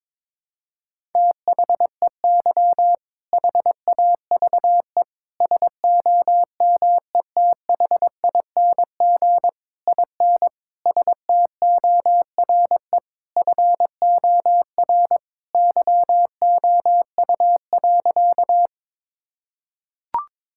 Morse code, heard as a keyed tone, about 22 wpm, 700 Hz. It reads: THEY HAVE SOMETHING IN STORE FOR YOU.
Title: Sentences from Top 1000 Words - Review Part 5 - 22wpm